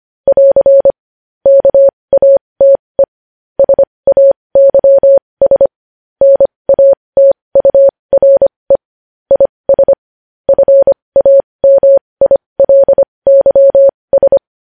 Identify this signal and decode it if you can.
Morse code, heard as a keyed tone, about 25 wpm, spaced as for 20 wpm, 550 Hz. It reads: <AR> KATE SAYS NATURE IS FAMILYS
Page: news headlines in morse code at 20 wpm